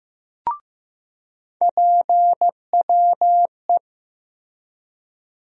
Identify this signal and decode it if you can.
Morse code, heard as a keyed tone, about 15 wpm, 700 Hz. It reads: PWE
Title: USA Call Sign Postfixes - 3 Letters 15wpm